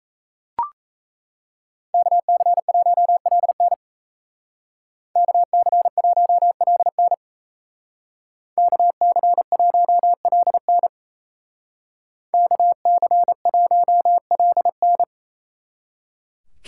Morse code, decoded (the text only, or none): KC1LN KC1LN KC1LN KC1LN